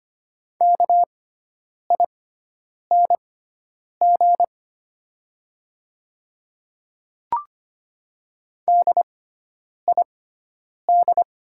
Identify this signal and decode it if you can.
Morse code, heard as a keyed tone, about 25 wpm, spaced as for 9 wpm, 690 Hz. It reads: KING DID
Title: 6x Character Spacing - Top 500 Words - 25wpm